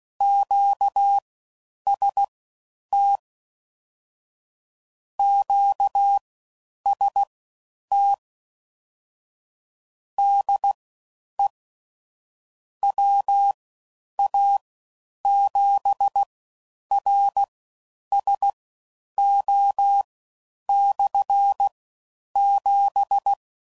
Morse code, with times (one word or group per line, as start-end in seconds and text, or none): 0.2-3.2 QST
5.2-8.1 QST
10.2-11.5 DE
12.8-23.3 WA7RSO/7